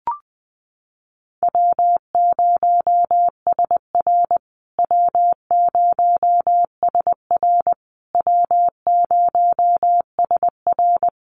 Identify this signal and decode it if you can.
Morse code, heard as a keyed tone, about 20 wpm, 700 Hz. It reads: W0SR W0SR W0SR
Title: USA Call Signs Repeated 3x 20wpm